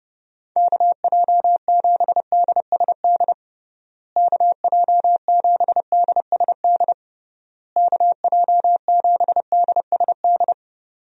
Morse code, heard as a keyed tone, about 30 wpm, 700 Hz. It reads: KJ7DSD KJ7DSD KJ7DSD
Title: USA Call Signs - 6 Characters - Repeated 3x - 30wpm